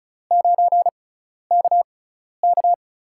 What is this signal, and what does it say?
Morse code, heard as a keyed tone, about 36 wpm, spaced as for 12 wpm, 700 Hz. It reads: 9KK
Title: USA Call Signs - 5 Characters - 6x Character Spacing - 35wpm